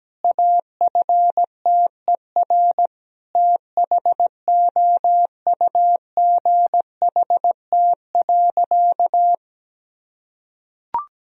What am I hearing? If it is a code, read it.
Morse code, heard as a keyed tone, about 17 wpm, 700 Hz. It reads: AFTER THOUGHT.
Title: Sentences from Top 1000 Words - Review Part 2 - 17wpm